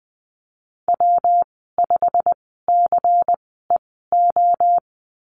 Morse code, heard as a keyed tone, about 20 wpm, spaced as for 14 wpm, 700 Hz. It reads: W5CEO
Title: 167 Lesson 27g - Number 7 - Call Signs 20wpm